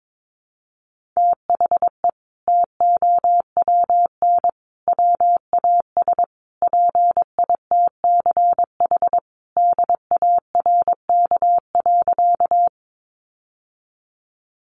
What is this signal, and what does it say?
Morse code, heard as a keyed tone, about 22 wpm, 700 Hz. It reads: THE TOWN WAS PITCH DARK.
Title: Sentences from Top 1300 Words 22wpm